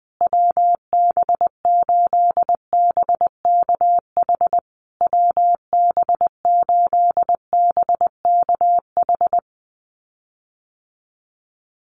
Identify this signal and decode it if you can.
Morse code, heard as a keyed tone, about 20 wpm, 700 Hz. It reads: WB8BKH WB8BKH
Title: USA Call Signs Repeated 3x 20wpm